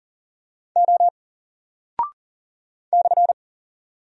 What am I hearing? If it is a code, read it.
Morse code, heard as a keyed tone, about 40 wpm, 700 Hz. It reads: O /
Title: Single Character (Letter:Number:Punctuation:Prosign) 40wpm